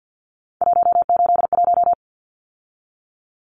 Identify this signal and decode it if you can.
Morse code, heard as a keyed tone, about 50 wpm, 700 Hz. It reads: 181
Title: Numbers - 3-Digit 50wpm